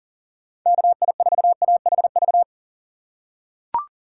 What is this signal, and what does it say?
Morse code, heard as a keyed tone, about 40 wpm, 700 Hz. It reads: KI4AHV
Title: USA Call Signs - 6 Characters - 6x Character Spacing - 40wpm